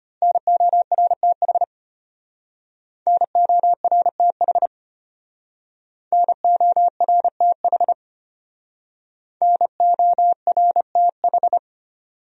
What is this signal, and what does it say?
Morse code, heard as a keyed tone, about 31 wpm, 700 Hz. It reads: NORTH NORTH NORTH NORTH
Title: Speed-Racing Top 500 Words 25wpm